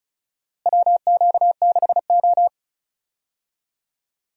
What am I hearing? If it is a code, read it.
Morse code, heard as a keyed tone, about 36 wpm, 700 Hz. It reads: WQ6O
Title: USA Call Signs - 4 Characters - Repeated 3x - 35wpm